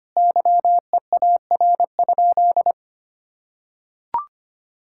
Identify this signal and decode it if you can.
Morse code, heard as a keyed tone, about 25 wpm, 700 Hz. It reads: YEAR?
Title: Sentences from Top 1000 Words - Review Part 5 - 25wpm